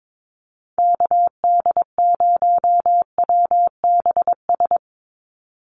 Morse code, heard as a keyed tone, about 22 wpm, 700 Hz. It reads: KD0WBS
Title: USA Call Signs 22wpm